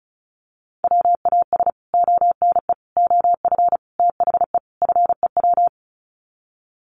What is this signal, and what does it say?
Morse code, heard as a keyed tone, about 34 wpm, 700 Hz. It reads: WAS ONE OF THE FEW
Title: Sets of 5 Words Encore 35wpm